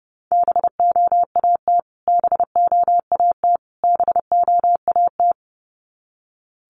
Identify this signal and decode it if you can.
Morse code, heard as a keyed tone, about 30 wpm, 700 Hz. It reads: BOAT BOAT BOAT
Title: Top 500 Words Repeated 30wpm